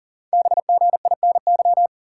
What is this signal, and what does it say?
Morse code, heard as a keyed tone, about 41 wpm, 690 Hz. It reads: BGINY
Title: Code Groups - 5 Letters - 40wpm